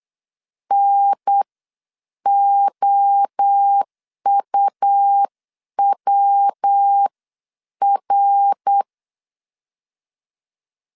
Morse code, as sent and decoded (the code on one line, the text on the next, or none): -. --- ..- .-- .-.
NOUWR